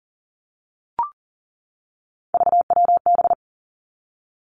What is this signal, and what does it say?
Morse code, heard as a keyed tone, about 38 wpm, 700 Hz. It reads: VWB
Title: Three Challenging Letter-Number Combinations 40wpm